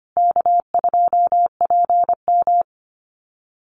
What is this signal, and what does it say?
Morse code, heard as a keyed tone, about 25 wpm, 700 Hz. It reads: K2PM
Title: USA Call Signs Repeated 3x 25wpm